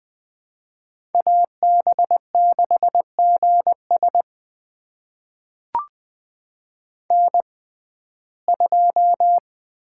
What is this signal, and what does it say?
Morse code, heard as a keyed tone, about 20 wpm, spaced as for 15 wpm, 700 Hz. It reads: AB6GS N2
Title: USA Call Signs - 5 Characters - 6x Character Spacing - 20wpm